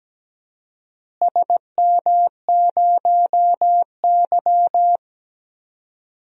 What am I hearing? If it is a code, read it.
Morse code, heard as a keyed tone, about 17 wpm, 700 Hz. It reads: SM0Y
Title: DXCC-100 Call Signs - 4 Characters - Repeated 2x - 17wpm